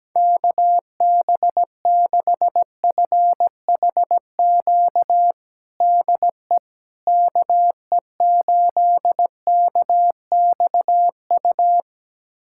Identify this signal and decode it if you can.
Morse code, heard as a keyed tone, about 17 wpm, 700 Hz. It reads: KB6FHQ DE KE8KXU
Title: USA Callsign DE Callsign 17wpm